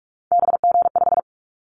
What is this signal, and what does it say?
Morse code, heard as a keyed tone, about 44 wpm, 700 Hz. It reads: BG5
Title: Callsign Prefixes for Top 100 DXCC - 45wpm